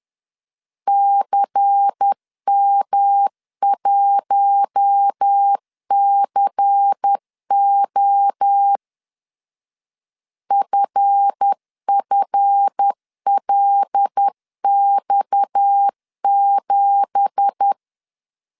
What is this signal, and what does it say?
Morse code, heard as a keyed tone, about 10 wpm, 800 Hz. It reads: CM1CO FFLX7
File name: CW / DET / 1 9 1500C S A 10wpm 4511